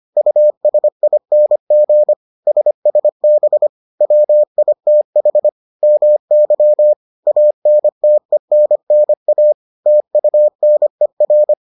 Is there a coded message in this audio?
Yes: Morse code from 0.2 to 11.5 s, heard as a keyed tone, about 25 wpm, 600 Hz. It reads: USING SSB WITH MY ANTENNA TUNER